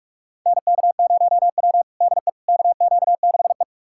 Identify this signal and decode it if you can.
Morse code, heard as a keyed tone, about 47 wpm, 700 Hz. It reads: NK0W DE KQ6E
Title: USA Callsign DE Callsign - 4 Characters - 45wpm